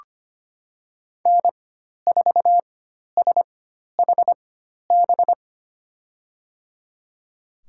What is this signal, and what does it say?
Morse code, heard as a keyed tone, about 25 wpm, spaced as for 12 wpm, 700 Hz. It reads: N4SHB
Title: USA Call Signs - 5 Characters - 4x Character Spacing - 25wpm